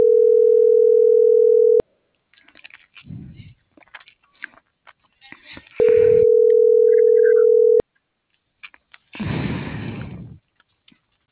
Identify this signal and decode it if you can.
Ringback tone (North America), 440 + 480 Hz.